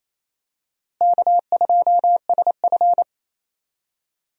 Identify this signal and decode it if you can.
Morse code, heard as a keyed tone, about 28 wpm, 700 Hz. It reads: K2SF
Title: USA Call Signs - 4 Characters - Repeated 3x - 28wpm